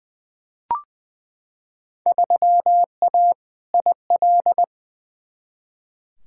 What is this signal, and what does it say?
Morse code, heard as a keyed tone, about 21 wpm, 700 Hz. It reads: 3A IL
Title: Field Day Exchanges - 20wpm